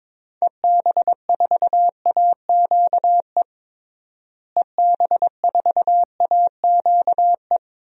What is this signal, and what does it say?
Morse code, heard as a keyed tone, about 22 wpm, 700 Hz. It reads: EB4AQE EB4AQE
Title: DXCC-100 Call Signs - 6 Characters - Repeated 2x - 22wpm